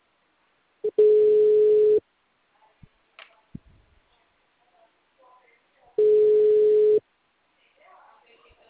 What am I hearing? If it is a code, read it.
Ringback tone (Europe, 425 Hz), with speech.
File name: out- -2015- - - .158